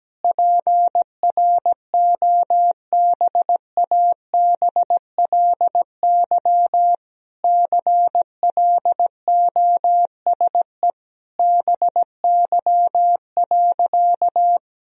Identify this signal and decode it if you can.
Morse code, heard as a keyed tone, about 17 wpm, 700 Hz. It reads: PROBABLY CLOSE BY.
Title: Sentences from Top 1000 Words - Review Part 4 - 17wpm